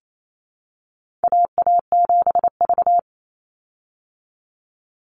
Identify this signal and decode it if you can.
Morse code, heard as a keyed tone, about 28 wpm, 700 Hz. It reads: AA7V
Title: Call Signs Repeated 2x - 28wpm